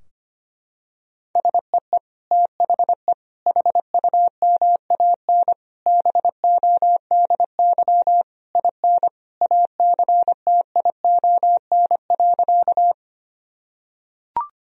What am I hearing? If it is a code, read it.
Morse code, heard as a keyed tone, about 25 wpm, 700 Hz. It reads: SEE THE HUMAN BODY IN ACTION.
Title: Sentences from Top 1000 Words - Review Part 4 - 25wpm